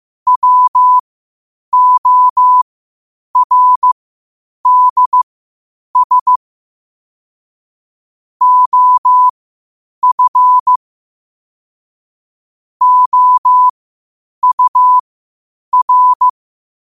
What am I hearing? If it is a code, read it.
Morse code, heard as a keyed tone, about 15 wpm, spaced as for 8 wpm, 1000 Hz. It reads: WORDS OF OUR
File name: qotdfiles / 8wpm